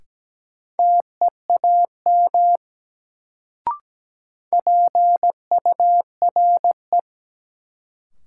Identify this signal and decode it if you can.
Morse code, heard as a keyed tone, about 17 wpm, 700 Hz. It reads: TEAM PURE
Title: Sentences from Top 1500 Words - New Word Drill - 17wpm